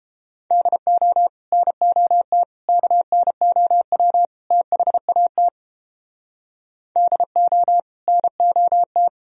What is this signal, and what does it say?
Morse code, heard as a keyed tone, about 30 wpm, 700 Hz. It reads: DO NOT KNOW THAT DO NOT